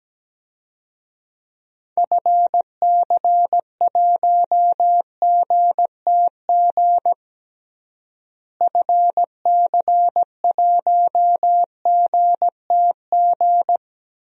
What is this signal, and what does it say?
Morse code, heard as a keyed tone, about 17 wpm, 700 Hz. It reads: FC1GTG FC1GTG
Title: DXCC-100 Call Signs - 6 Characters - Repeated 3x - 17wpm